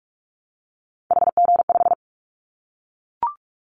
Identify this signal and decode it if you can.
Morse code, heard as a keyed tone, about 44 wpm, 700 Hz. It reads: HG5
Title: Callsign Prefixes for Top 25 DXCC - Repeated 3x - 45wpm